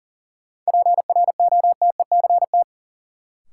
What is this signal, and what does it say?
Morse code, heard as a keyed tone, about 40 wpm, 700 Hz. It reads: PROTECT
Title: Sentences from Top 900 Words - New Word Drill - 40wpm